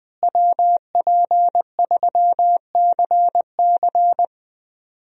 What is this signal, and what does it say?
Morse code, heard as a keyed tone, about 20 wpm, 700 Hz. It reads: WP3CC
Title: USA Call Signs - 5 Characters - Repeated 2x - 20wpm